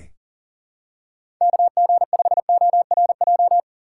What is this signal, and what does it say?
Morse code, heard as a keyed tone, about 40 wpm, 690 Hz. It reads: KG5ORJ
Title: USA Call Signs - 3x Character Spacing - 40wpm